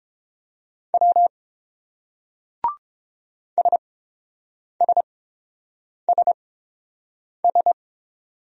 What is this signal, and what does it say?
Morse code, heard as a keyed tone, about 32 wpm, spaced as for 7 wpm, 700 Hz. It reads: W SSSS